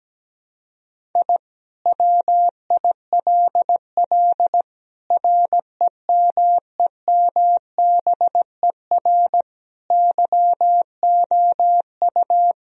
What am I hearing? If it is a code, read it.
Morse code, heard as a keyed tone, about 17 wpm, 700 Hz. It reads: I WILL REMEMBER YOU
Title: Sentences from Top 1000 Words - Review Part 2 - 17wpm